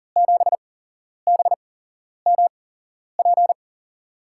Morse code, heard as a keyed tone, about 40 wpm, spaced as for 11 wpm, 700 Hz. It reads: 7BMP
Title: USA Call Signs - 6 Characters - 8x Character Spacing - 40wpm